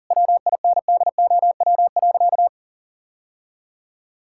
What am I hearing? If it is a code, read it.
Morse code, heard as a keyed tone, about 40 wpm, 700 Hz. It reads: WINDOW.
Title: Sentences from Top 1600 Words 40wpm